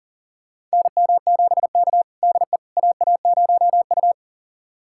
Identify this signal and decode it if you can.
Morse code, heard as a keyed tone, about 40 wpm, 700 Hz. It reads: NM7K DE AA0U
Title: USA Callsign DE Callsign - 4 Characters - 40wpm